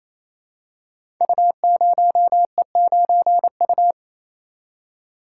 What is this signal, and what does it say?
Morse code, heard as a keyed tone, about 27 wpm, 700 Hz. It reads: U0E9U
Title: Code Groups - 5 Letter:Numbers - 28wpm